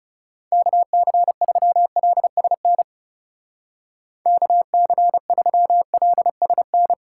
Morse code, heard as a keyed tone, about 37 wpm, 700 Hz. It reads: KC3LSN KC3LSN
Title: Speed Racing - USA 2x3 Call Signs 30wpm